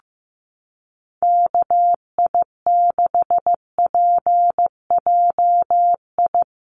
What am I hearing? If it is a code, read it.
Morse code, heard as a keyed tone, about 14 wpm, 700 Hz. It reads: KI6PJI
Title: USA Call Signs Repeated 3x 15wpm